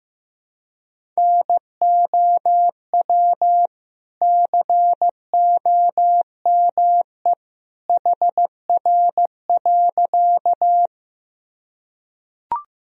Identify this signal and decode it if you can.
Morse code, heard as a keyed tone, about 16 wpm, 700 Hz. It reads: NOW COME HR.
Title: Sentences from Top 200 Words and Abbreviations 15wpm